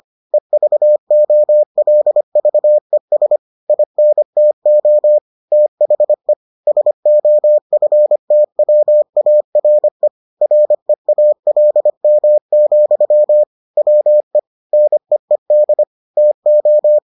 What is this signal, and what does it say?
Morse code, heard as a keyed tone, about 25 wpm, 600 Hz. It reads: EVOLVES INTO THE SOFTWARE REALM, WE NEED TO